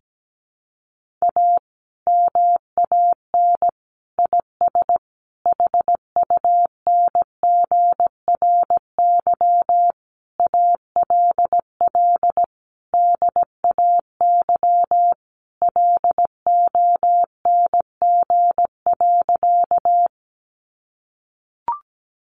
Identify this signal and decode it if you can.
Morse code, heard as a keyed tone, about 17 wpm, 700 Hz. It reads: A MAN IS HUNGRY ALL DAY LONG.
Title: Sentences from Top 1400 Words 17wpm